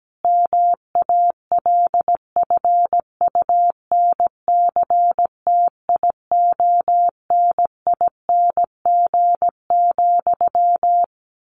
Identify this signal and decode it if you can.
Morse code, heard as a keyed tone, about 17 wpm, 700 Hz. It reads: MALFUNCTIONING,
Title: A Sound Of Thunder 17wpm